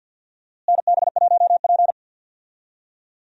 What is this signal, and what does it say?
Morse code, heard as a keyed tone, about 52 wpm, 700 Hz. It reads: NB1P
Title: USA Call Signs - 4 Characters - Repeated 3x - 50wpm